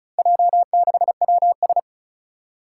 Morse code, heard as a keyed tone, about 36 wpm, 700 Hz. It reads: J6WS
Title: Four Challenging Letter-Number Combinations 35wpm